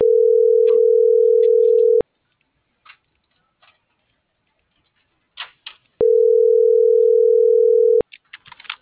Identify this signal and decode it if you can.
Ringback tone (North America), 440 + 480 Hz.